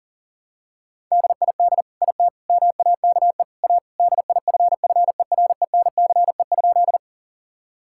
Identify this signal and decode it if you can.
Morse code, heard as a keyed tone, about 40 wpm, 700 Hz. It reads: DID IT MAKE A DIFFERENCE?
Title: Sentences from Top 1000 Words - Review Part 5 - 40wpm